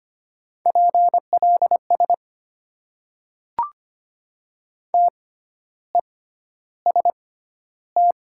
Morse code, heard as a keyed tone, about 25 wpm, spaced as for 12 wpm, 700 Hz. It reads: PLS TEST
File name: Top 90 QSO Elements - 6x Character Spacing - 25wpm